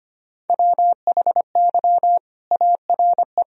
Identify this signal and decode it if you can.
Morse code, heard as a keyed tone, about 25 wpm, 700 Hz. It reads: WHY ARE
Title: Sentences from Top 1000 Words - Review Part 2 - 25wpm